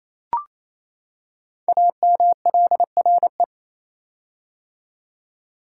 Morse code, heard as a keyed tone, about 28 wpm, 700 Hz. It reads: AMLRE
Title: Code Groups - 5 Letters - 28wpm